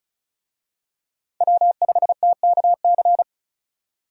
Morse code, heard as a keyed tone, about 34 wpm, 700 Hz. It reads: W5TKC
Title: USA Call Signs - 5 Characters - 35wpm